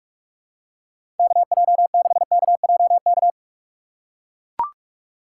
Morse code, heard as a keyed tone, about 45 wpm, 700 Hz. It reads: KJ6KJK
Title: USA Call Signs - 6 Characters - 2x Character Spacing - 45wpm